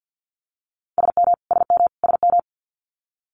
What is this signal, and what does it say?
Morse code, heard as a keyed tone, about 50 wpm, 700 Hz. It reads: SM SM SM
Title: Multi-letter Phonemes Repeated 3x 50wpm